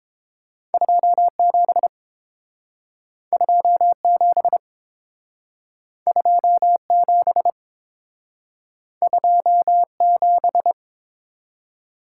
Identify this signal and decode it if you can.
Morse code, heard as a keyed tone, about 29 wpm, 700 Hz. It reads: 27 27 27 27